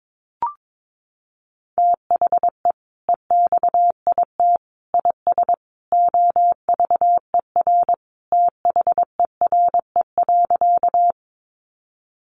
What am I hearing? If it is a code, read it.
Morse code, heard as a keyed tone, about 22 wpm, 700 Hz. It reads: THE EXIT IS OVER THERE.
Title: Sentences from Top 1100 Words 22wpm